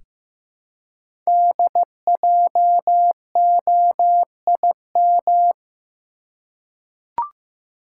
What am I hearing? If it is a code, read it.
Morse code, heard as a keyed tone, about 15 wpm, 700 Hz. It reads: DJOIM
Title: Code Groups - 5 Letters - 15wpm